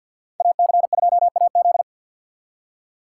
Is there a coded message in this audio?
Yes: Morse code from 0.4 to 1.8 s, heard as a keyed tone, about 63 wpm, 700 Hz. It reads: AC2AZ